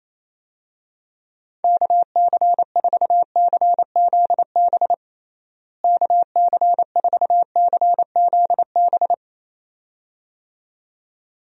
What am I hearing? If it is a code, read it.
Morse code, heard as a keyed tone, about 28 wpm, 700 Hz. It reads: KC4CZB KC4CZB
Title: USA Call Signs - 6 Characters - Repeated 2x - 28wpm